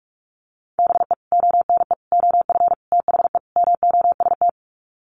Morse code, heard as a keyed tone, about 44 wpm, 700 Hz. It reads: BE ONE OF THE MOST
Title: Sets of 5 Words Encore 45wpm